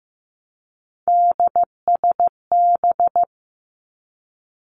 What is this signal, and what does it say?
Morse code, heard as a keyed tone, about 15 wpm, 700 Hz. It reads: DSB
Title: Three Challenging Letter-Number Combinations 15wpm